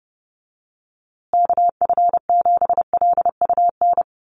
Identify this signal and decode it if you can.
Morse code, heard as a keyed tone, about 30 wpm, 700 Hz. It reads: KF7LUN
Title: USA Call Signs Repeated 3x 30wpm